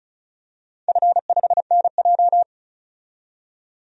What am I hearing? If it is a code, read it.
Morse code, heard as a keyed tone, about 34 wpm, 700 Hz. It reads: F5NJ
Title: DXCC-100 Call Signs - 4 Characters - 35wpm